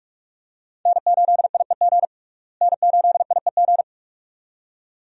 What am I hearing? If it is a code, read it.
Morse code, heard as a keyed tone, about 46 wpm, 700 Hz. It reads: N8IEG N8IEG
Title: USA Call Signs - 5 Characters - Repeated 2x - 45wpm